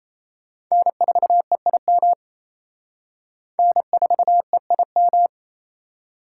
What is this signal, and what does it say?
Morse code, heard as a keyed tone, about 37 wpm, 700 Hz. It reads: N4EIM N4EIM